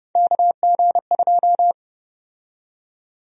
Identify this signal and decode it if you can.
Morse code, heard as a keyed tone, about 30 wpm, 700 Hz. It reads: KG2
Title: USA Call Sign Prefixes - 30wpm